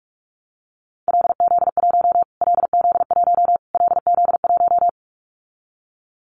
Morse code, heard as a keyed tone, about 44 wpm, 700 Hz. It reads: LZ1 LZ1 LZ1
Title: Callsign Prefixes for Top 25 DXCC - Repeated 3x - 45wpm